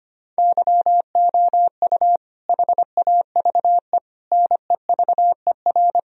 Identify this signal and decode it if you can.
Morse code, heard as a keyed tone, about 25 wpm, 700 Hz. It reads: YOU HAVE NEVER
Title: Sentences from Top 1000 Words - Review Part 2 - 25wpm